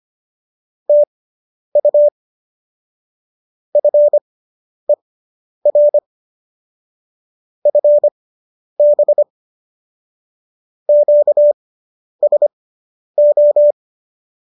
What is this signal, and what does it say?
Morse code, heard as a keyed tone, about 25 wpm, spaced as for 10 wpm, 590 Hz. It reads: TU FER FB QSO